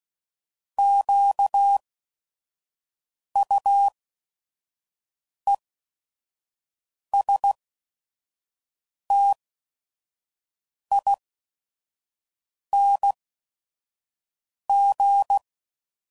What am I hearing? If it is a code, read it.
Morse code, heard as a keyed tone, about 16 wpm, spaced as for 5 wpm, 800 Hz. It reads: QUESTING
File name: Prefixes / By First Letter / Qs